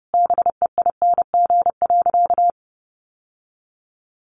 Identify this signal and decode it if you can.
Morse code, heard as a keyed tone, about 30 wpm, 700 Hz. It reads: BEING.
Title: Sentences from Top 1000 Words 30wpm